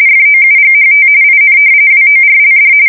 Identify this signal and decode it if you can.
RTTY, about 45.45 baud. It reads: FOUGHT IN WORLD
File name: fsk170 / battleshipnj